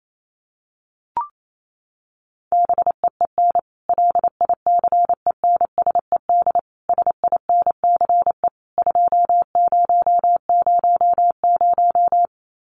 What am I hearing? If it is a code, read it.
Morse code, heard as a keyed tone, about 28 wpm, 700 Hz. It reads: BEEN LICENSED SINCE 2000
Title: Common QSO Phrases 28wpm